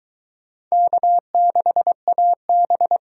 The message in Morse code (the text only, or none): K6AB